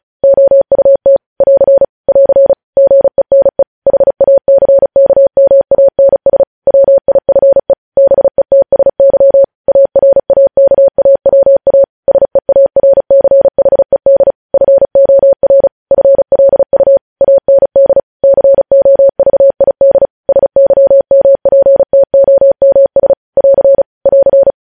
Morse code, heard as a keyed tone, about 36 wpm, 550 Hz. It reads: OUT <AR> <AR> GENE HACKMANS WIFE BETSY ARAKAWA SEARCHED FOR FLU AND COVID SYMPTOMS <AR> <AR>
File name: podcast / News headlines-1x-35x35